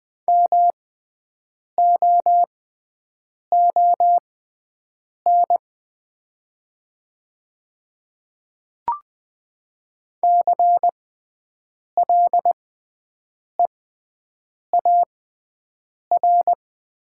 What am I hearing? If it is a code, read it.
Morse code, heard as a keyed tone, about 20 wpm, spaced as for 7 wpm, 700 Hz. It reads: MOON CLEAR
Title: 6x Character Spacing - Top 500 Words - 20wpm